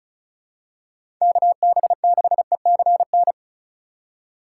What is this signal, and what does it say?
Morse code, heard as a keyed tone, about 36 wpm, 700 Hz. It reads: KB6ECN
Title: USA Call Signs - 6 Characters - 3x Character Spacing - 35wpm